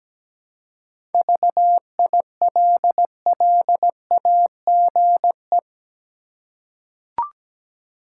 Morse code, heard as a keyed tone, about 17 wpm, 700 Hz. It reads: VILLAGE
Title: Seven Letter Words - 17wpm